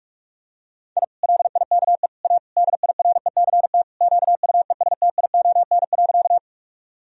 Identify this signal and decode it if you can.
Morse code, heard as a keyed tone, about 46 wpm, 700 Hz. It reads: I LIKE A DIRECT QUESTION.
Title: Sentences from Top 1000 Words - Review Part 2 - 45wpm